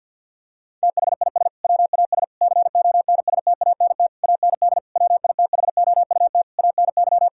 Morse code, heard as a keyed tone, about 50 wpm, 700 Hz. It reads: THIS WAS CONSTANT AND WITHOUT ANY
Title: Sentences from Top 1600 Words 50wpm